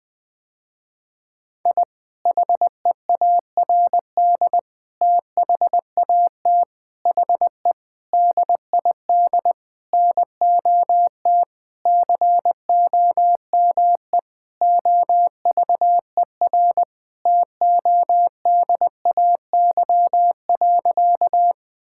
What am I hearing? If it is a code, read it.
Morse code, heard as a keyed tone, about 20 wpm, 700 Hz. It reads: I HEARD THAT HE DID NOT COME OVER TODAY.
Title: Sentences with 4-Word Phrases - Spoken Beginning - 20wpm